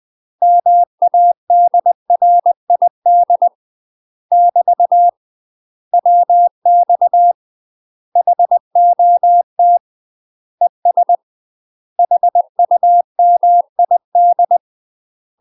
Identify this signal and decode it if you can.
Morse code, heard as a keyed tone, about 20 wpm, 700 Hz. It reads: MADRID <BT> WX HOT ES HUMID
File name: UK MASTER QSOs TRAINING / 20WPM / UK MASTER QSOs 0004